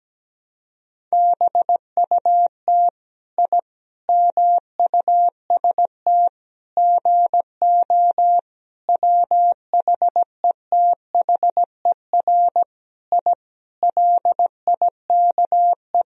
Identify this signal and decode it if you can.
Morse code, heard as a keyed tone, about 17 wpm, 700 Hz. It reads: BUT I MUST GO WHETHER I LIKE
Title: Sentences from Top 1000 Words - Review Part 3 - 17wpm